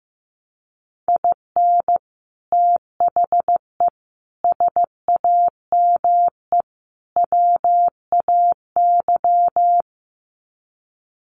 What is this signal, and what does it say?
Morse code, heard as a keyed tone, about 15 wpm, 700 Hz. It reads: IN THE SAME WAY
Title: Sets of 4 Words 15wpm